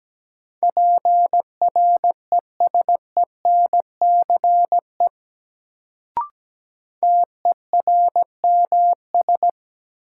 Morse code, heard as a keyed tone, about 17 wpm, 700 Hz. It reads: PRESENCE TERMS
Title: Sets of 3 Words - New Word Drill - 17wpm